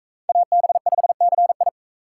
Morse code, heard as a keyed tone, about 41 wpm, 700 Hz. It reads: AB5CI